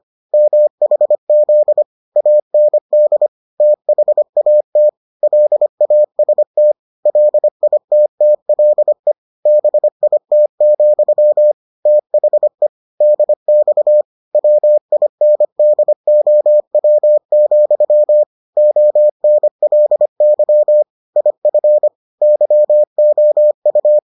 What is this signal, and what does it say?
Morse code, heard as a keyed tone, about 25 wpm, 600 Hz. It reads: MHZ AND THAT LAST LITTLE BIT, THE DX WINDOW, ONLY IF YOU